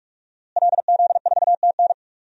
Morse code, heard as a keyed tone, about 46 wpm, 700 Hz. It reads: LZ4TN
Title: DXCC-100 Call Signs - 5 Characters - Repeated 3x - 45wpm